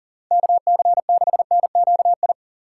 Morse code, heard as a keyed tone, about 53 wpm, 700 Hz. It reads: KC6NQI